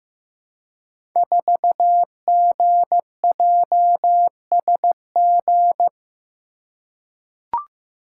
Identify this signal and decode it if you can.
Morse code, heard as a keyed tone, about 15 wpm, 700 Hz. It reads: 4GJSG
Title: Code Groups - 5 Letter:Numbers - 15wpm